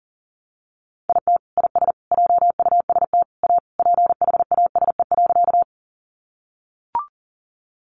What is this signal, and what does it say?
Morse code, heard as a keyed tone, about 37 wpm, 700 Hz. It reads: IT IS JUST A PHASE.
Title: Sentences from Top 1200 Words 40wpm